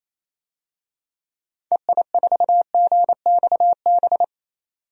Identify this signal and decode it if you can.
Morse code, heard as a keyed tone, about 28 wpm, 700 Hz. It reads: EI4GXB
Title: DXCC-100 Call Signs - 6 Characters - 28wpm